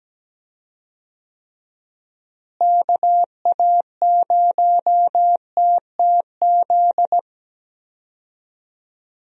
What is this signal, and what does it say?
Morse code, heard as a keyed tone, about 17 wpm, 700 Hz. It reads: KA0TTZ
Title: USA Call Signs - 6 Characters - Repeated 2x - 17wpm